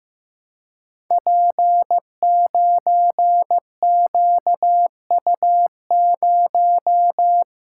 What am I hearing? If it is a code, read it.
Morse code, heard as a keyed tone, about 15 wpm, 700 Hz. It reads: P9QU0
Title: Code Groups - 5 Letter:Numbers - 15wpm